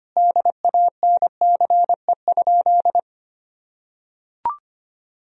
Morse code, heard as a keyed tone, about 25 wpm, 700 Hz. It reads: DANCE?
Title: Sentences from Top 1000 Words - Review Part 4 - 25wpm